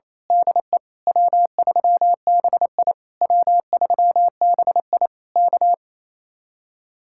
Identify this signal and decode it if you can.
Morse code, heard as a keyed tone, about 26 wpm, 700 Hz. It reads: DE W3BI W3BI K
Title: Speed-Racing Common QSO Phrases - 28wpm